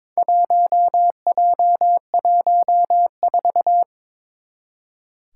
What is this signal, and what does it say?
Morse code, heard as a keyed tone, about 22 wpm, 700 Hz. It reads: 1J14
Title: Four Challenging Letter-Number Combinations 22wpm